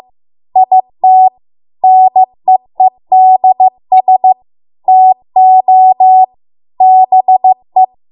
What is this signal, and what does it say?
Morse code, heard as a keyed tone, about 15 wpm, 750 Hz. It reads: IT NEEDS TO BE